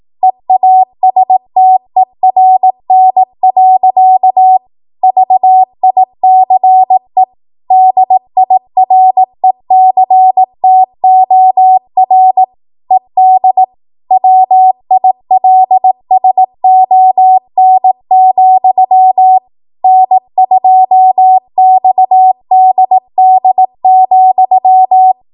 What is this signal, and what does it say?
Morse code, heard as a keyed tone, about 18 wpm, 750 Hz. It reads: EASTERN. VICE DIRECTOR ED WILSON, N2XDD,
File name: Archive / 18 WPM / 18WPM